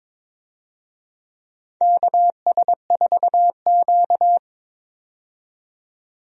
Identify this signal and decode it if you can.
Morse code, heard as a keyed tone, about 22 wpm, 700 Hz. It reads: KS4Q
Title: USA Call Signs - 4 Characters - 22wpm